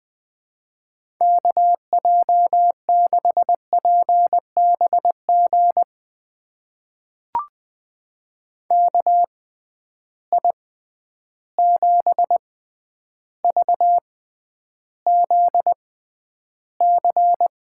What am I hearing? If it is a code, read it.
Morse code, heard as a keyed tone, about 21 wpm, spaced as for 10 wpm, 700 Hz. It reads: KJ6PBG KI7VZC
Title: USA Call Signs - 6 Characters - 6x Character Spacing - 20wpm